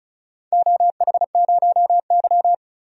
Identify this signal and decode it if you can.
Morse code, heard as a keyed tone, about 34 wpm, 700 Hz. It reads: OH0Y
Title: DXCC-100 Call Signs - 4 Characters - 35wpm